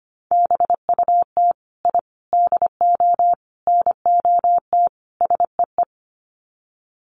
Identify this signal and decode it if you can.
Morse code, heard as a keyed tone, about 25 wpm, 700 Hz. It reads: BUT I DO NOT SEE
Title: Sets of 5 Words Encore 25wpm